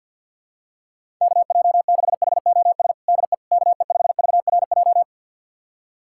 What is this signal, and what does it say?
Morse code, heard as a keyed tone, about 52 wpm, 700 Hz. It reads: KJ6HOS DE KE5VRJ
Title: USA Callsign DE Callsign - 6 Characters - 50wpm